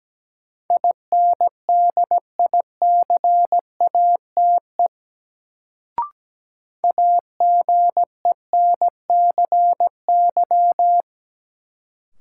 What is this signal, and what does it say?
Morse code, heard as a keyed tone, about 17 wpm, 700 Hz. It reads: INDICATE AGENCY
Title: Sentences from Top 1300 Words - New Word Drill - 17wpm